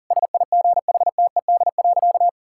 Morse code, heard as a keyed tone, about 40 wpm, 700 Hz. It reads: SIGHTED.
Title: Pangram Challenge v2 - 40wpm